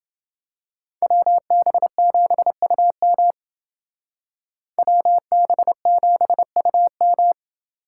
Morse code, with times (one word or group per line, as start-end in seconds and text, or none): 1.0-3.3 WB7UM
4.8-7.3 WB7UM